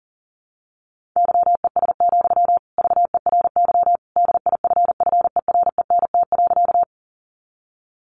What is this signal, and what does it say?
Morse code, heard as a keyed tone, about 40 wpm, 700 Hz. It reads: YES, VERY DIFFERENT.
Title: Sentences from Top 600 Words 40wpm